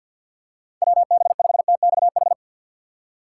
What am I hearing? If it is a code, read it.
Morse code, heard as a keyed tone, about 47 wpm, 690 Hz. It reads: WB5TXH
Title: USA Call Signs - 2x Character Spacing - 50wpm